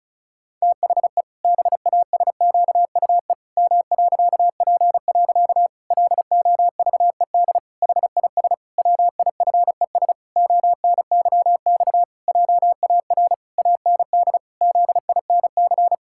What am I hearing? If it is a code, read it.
Morse code, heard as a keyed tone, about 34 wpm, 700 Hz. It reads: THE BASQUE M.P. LOVED HIS WIFES ONYX JAR AND ZINC
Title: Pangram Challenge v2 - 35wpm